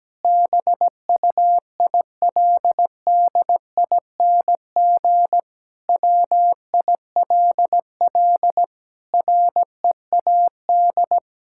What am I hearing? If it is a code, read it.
Morse code, heard as a keyed tone, about 17 wpm, 700 Hz. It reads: BUILDING WILL READ
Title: Sentences from Top 1000 Words - Review Part 6 - 17wpm